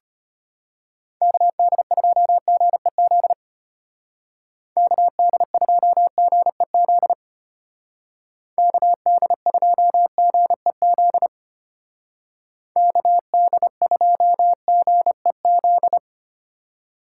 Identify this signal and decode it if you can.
Morse code, heard as a keyed tone, about 30 wpm, 700 Hz. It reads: KD2GEZ KD2GEZ KD2GEZ KD2GEZ
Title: Speed Racing - USA 2x3 Call Signs 25wpm